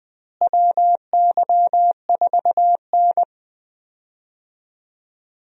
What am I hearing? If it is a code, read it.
Morse code, heard as a keyed tone, about 20 wpm, 700 Hz. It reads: WY4N
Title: USA Call Signs - 4 Characters - Repeated 2x - 20wpm